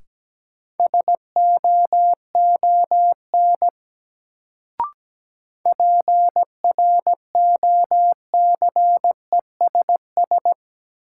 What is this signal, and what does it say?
Morse code, heard as a keyed tone, about 17 wpm, 700 Hz. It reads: SOON PROCESS
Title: Sets of 3 Words - New Word Drill - 17wpm